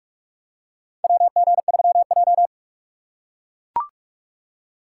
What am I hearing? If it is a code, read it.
Morse code, heard as a keyed tone, about 43 wpm, 700 Hz. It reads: WG3J
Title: USA Call Signs - 4 Characters - 3x Character Spacing - 45wpm